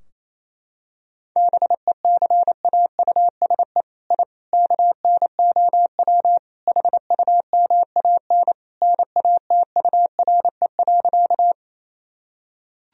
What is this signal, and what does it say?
Morse code, heard as a keyed tone, about 27 wpm, 700 Hz. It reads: BECAUSE I KNOW HUMAN NATURE.
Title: Sentences from Top 1000 Words - Review Part 3 - 28wpm